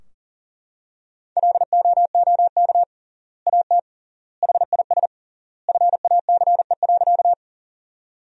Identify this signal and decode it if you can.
Morse code, heard as a keyed tone, about 37 wpm, 700 Hz. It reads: LOOK AT HIS FACE.
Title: 3x Word-Spacing - Sentences from Top 300 Words - 40wpm